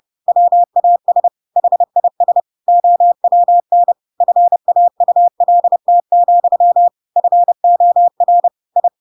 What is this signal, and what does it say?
Morse code, heard as a keyed tone, about 30 wpm, 700 Hz. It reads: WAS HIS OWN FAULT, FOR I
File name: Pinocchio / Pinocchio ch7 30 30 ik1hge